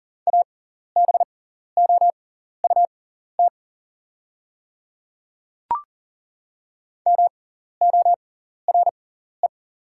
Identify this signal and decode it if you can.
Morse code, heard as a keyed tone, about 40 wpm, spaced as for 14 wpm, 700 Hz. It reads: ABOUT MORE
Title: 6x Character Spacing - Top 100 Words - 40wpm